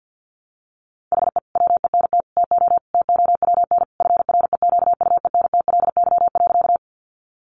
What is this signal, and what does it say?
Morse code, heard as a keyed tone, about 50 wpm, 700 Hz. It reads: HE WENT TO TOWN FREQUENTLY.
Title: Sentences from Top 1300 Words 50wpm